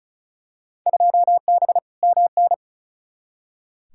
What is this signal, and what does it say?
Morse code, heard as a keyed tone, about 36 wpm, 700 Hz. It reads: 2B MN
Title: Field Day Exchanges - 35wpm